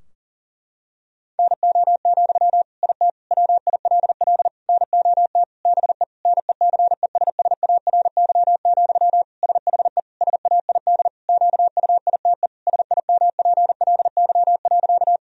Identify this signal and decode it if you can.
Morse code, heard as a keyed tone, about 40 wpm, 700 Hz. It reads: NO, IT WILL NOT BE NECESSARY, SHE SAID QUITE SIMPLY.
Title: Sentences from Top 1000 Words - Review Part 4 - 40wpm